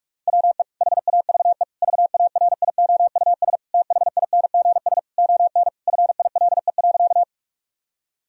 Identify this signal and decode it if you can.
Morse code, heard as a keyed tone, about 46 wpm, 700 Hz. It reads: WE HAVE VARIOUS THINGS ON FILE.
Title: Sentences from Top 1000 Words - Review Part 4 - 45wpm